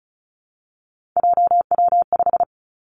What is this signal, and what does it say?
Morse code, heard as a keyed tone, about 36 wpm, 700 Hz. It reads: JW5
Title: Three Challenging Letter-Number Combinations 35wpm